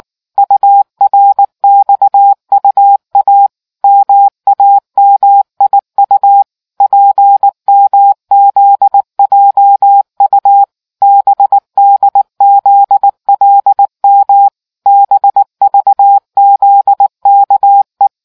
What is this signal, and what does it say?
Morse code, heard as a keyed tone, about 19 wpm, 800 Hz. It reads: URXUA MAMIU PMZJU BDZLM BVZKE